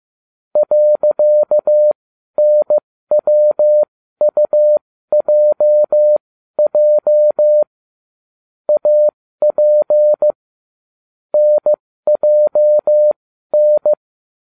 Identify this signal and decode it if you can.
Morse code, heard as a keyed tone, about 15 wpm, spaced as for 12 wpm, 600 Hz. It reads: .NWUJJ AP NJN